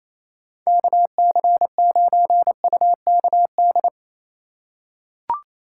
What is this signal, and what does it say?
Morse code, heard as a keyed tone, about 28 wpm, 700 Hz. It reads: KC9UKD
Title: USA Call Signs - 6 Characters - 3x Character Spacing - 28wpm